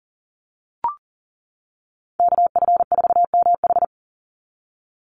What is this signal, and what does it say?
Morse code, heard as a keyed tone, about 40 wpm, 700 Hz. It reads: KF4MH
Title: NA SOTA Chaser Callsigns 40wpm